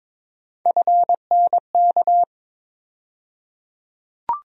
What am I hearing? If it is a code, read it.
Morse code, heard as a keyed tone, about 22 wpm, 700 Hz. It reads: FNK
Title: USA Call Sign Postfixes - 3 Letters 22wpm